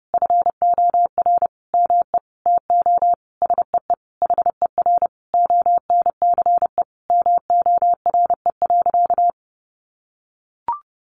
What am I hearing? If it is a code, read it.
Morse code, heard as a keyed tone, about 30 wpm, 700 Hz. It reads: FOR ME TO SEE HER ONCE MORE.
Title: Sentences from Top 500 Words 30wpm